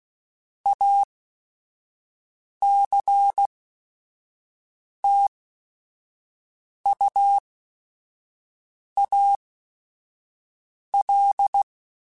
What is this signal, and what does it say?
Morse code, heard as a keyed tone, about 16 wpm, spaced as for 5 wpm, 800 Hz. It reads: ACTUAL
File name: Technical Subjects / Technical Subjects-mp3s / XL @5